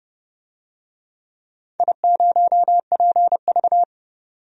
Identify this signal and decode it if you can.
Morse code, heard as a keyed tone, about 30 wpm, 700 Hz. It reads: I0PV
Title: DXCC-100 Call Signs - 4 Characters - 30wpm